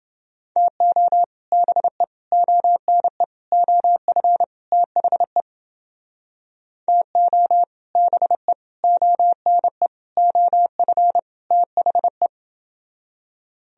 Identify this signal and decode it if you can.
Morse code, heard as a keyed tone, about 29 wpm, 700 Hz. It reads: TO BE ONE OF THE TO BE ONE OF THE